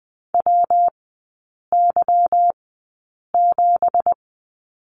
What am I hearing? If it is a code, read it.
Morse code, heard as a keyed tone, about 20 wpm, spaced as for 8 wpm, 700 Hz. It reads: WY7
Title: 212 Lesson 34e - Number 8 - Three Characters 20wpm